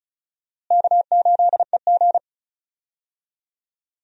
Morse code, heard as a keyed tone, about 36 wpm, 700 Hz. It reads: K8EG
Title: USA Call Signs - 4 Characters - Repeated 3x - 35wpm